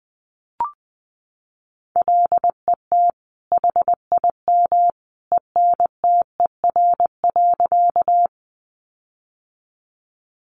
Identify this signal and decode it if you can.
Morse code, heard as a keyed tone, about 20 wpm, 700 Hz. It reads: LET HIM ENTER.
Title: Sentences from Top 1300 Words 20wpm